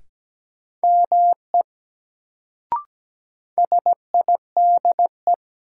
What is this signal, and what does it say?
Morse code, heard as a keyed tone, about 17 wpm, 700 Hz. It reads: ME SIDE
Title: Sets of 2 Words - New Word Drill - 17wpm